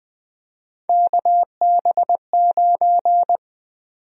Morse code, heard as a keyed tone, about 20 wpm, 700 Hz. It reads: KB9
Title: USA Call Sign Prefixes - 20wpm